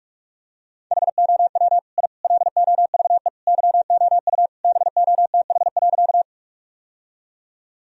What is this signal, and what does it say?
Morse code, heard as a keyed tone, about 46 wpm, 700 Hz. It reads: HOW I LOVE YOU BOTH.
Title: Sentences from Top 1000 Words - Review Part 1 - 45wpm